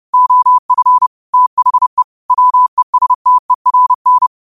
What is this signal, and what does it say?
Morse code, heard as a keyed tone, about 30 wpm, 1000 Hz. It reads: OF THE WESTERN